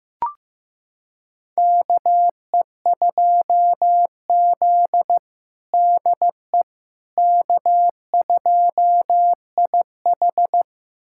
Morse code, heard as a keyed tone, about 15 wpm, 700 Hz. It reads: KE2Z DE K2IH
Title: USA Callsign DE Callsign - 4 Characters - 15wpm